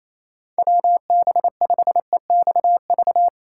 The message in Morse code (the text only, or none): WB5EXV